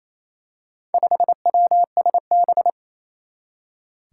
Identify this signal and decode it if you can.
Morse code, heard as a keyed tone, about 28 wpm, 700 Hz. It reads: 5WSB
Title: Four Challenging Letter-Number Combinations 28wpm